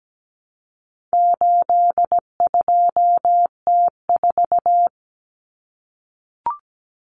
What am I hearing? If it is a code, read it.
Morse code, heard as a keyed tone, about 17 wpm, 700 Hz. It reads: 82T4
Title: Common Cut Numbers - 4-Digit 17wpm